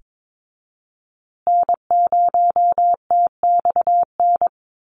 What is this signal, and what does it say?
Morse code, heard as a keyed tone, about 22 wpm, 700 Hz. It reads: N0TXN
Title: USA Call Signs Repeated 2x 22wpm